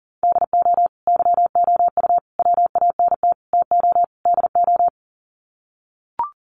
Morse code, heard as a keyed tone, about 39 wpm, 700 Hz. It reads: DO YOU WANT TO DO
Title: Sets of 5 Words Encore 40wpm